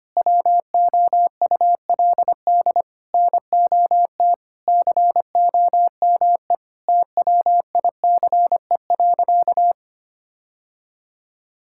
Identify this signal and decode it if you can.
Morse code, heard as a keyed tone, about 25 wpm, 700 Hz. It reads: WOULD NOT COME TWICE.
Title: Sentences from Top 1600 Words 25wpm